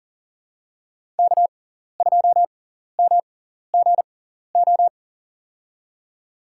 Morse code, heard as a keyed tone, about 40 wpm, spaced as for 14 wpm, 700 Hz. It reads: K2MGO
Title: USA Call Signs - 5 Characters - 6x Character Spacing - 40wpm